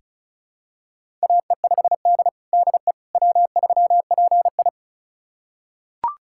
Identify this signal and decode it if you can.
Morse code, heard as a keyed tone, about 34 wpm, 700 Hz. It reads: AE5D DE W3PI
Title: USA Callsign DE Callsign - 4 Characters - 35wpm